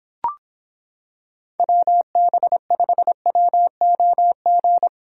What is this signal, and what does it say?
Morse code, heard as a keyed tone, about 26 wpm, 700 Hz. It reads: WB5WOG